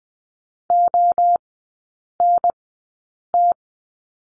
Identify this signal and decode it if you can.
Morse code, heard as a keyed tone, about 20 wpm, spaced as for 8 wpm, 700 Hz. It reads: ONT
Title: 047 Lesson 9e - Number 4 - Three Characters 20wpm